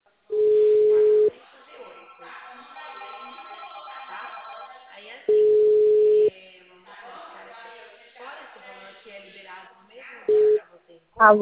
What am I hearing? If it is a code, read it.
Ringback tone (Europe, 425 Hz), with speech.
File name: out- -2044- - - .18432